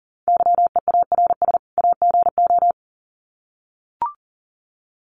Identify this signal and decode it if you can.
Morse code, heard as a keyed tone, about 40 wpm, 700 Hz. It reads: YEARS AGO
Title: Sets of 2 Words 40wpm